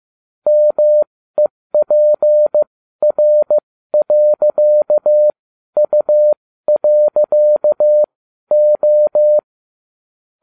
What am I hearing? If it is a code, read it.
Morse code, heard as a keyed tone, about 15 wpm, spaced as for 12 wpm, 600 Hz. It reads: MEPR.U.O
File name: cw / ubacw20 3